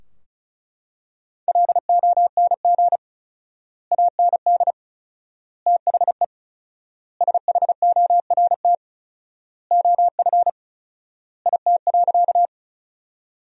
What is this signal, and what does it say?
Morse code, heard as a keyed tone, about 34 wpm, 700 Hz. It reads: LONG AND THE SHORT OF IT.
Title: 4x Word-Spacing - Sentences from Top 400 Words - 35wpm